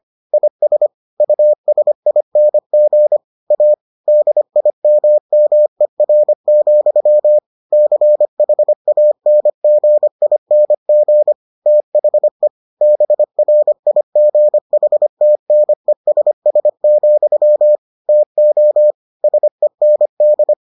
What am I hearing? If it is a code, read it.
Morse code, heard as a keyed tone, about 25 wpm, 600 Hz. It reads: IS USING A DIMMER, CHANGING THE BRIGHTNESS, TO SEND